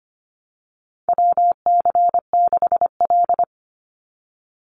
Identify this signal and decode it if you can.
Morse code, heard as a keyed tone, about 25 wpm, 700 Hz. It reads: WC6L
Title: USA Call Signs 25wpm